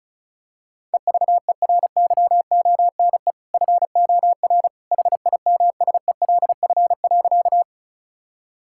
Morse code, heard as a keyed tone, about 36 wpm, 690 Hz. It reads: EVERYONE FOR HIMSELF.
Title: Sentences from Top 1600 Words 35wpm